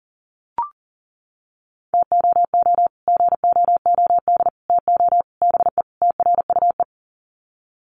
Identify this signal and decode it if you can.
Morse code, heard as a keyed tone, about 40 wpm, 700 Hz. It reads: TOO GOOD TO BE TRUE
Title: Sets of 5 Words Encore 40wpm